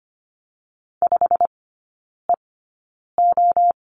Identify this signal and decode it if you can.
Morse code, heard as a keyed tone, about 25 wpm, spaced as for 9 wpm, 700 Hz. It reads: 5EO
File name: Lesson 17e - Letter u - Three Characters 25wpm